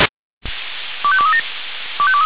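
Five-tone selective call (ZVEI), heard as quick runs of five tones, tones of about 70 ms. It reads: 25237
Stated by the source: ZVEI five-tone selective call received on an SDR.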